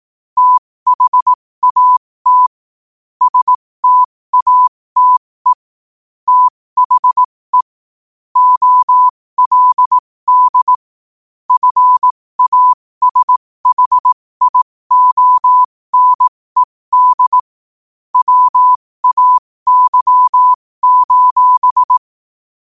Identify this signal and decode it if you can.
Morse code, heard as a keyed tone, about 18 wpm, spaced as for 15 wpm, 1000 Hz. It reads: THAT STATE THE OLD FASHIONED WAY:
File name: qotdfiles / 15wpm